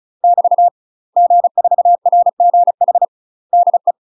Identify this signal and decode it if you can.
Morse code, heard as a keyed tone, about 34 wpm, 700 Hz. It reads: <BT> G4RGH DE
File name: UK MASTER QSOs TRAINING / 35WPM / UK MASTER QSOs 0019